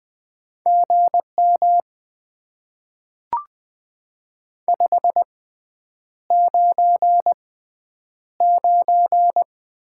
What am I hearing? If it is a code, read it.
Morse code, heard as a keyed tone, about 20 wpm, spaced as for 9 wpm, 700 Hz. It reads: GM 599
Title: Top 90 QSO Elements - 6x Character Spacing - 20wpm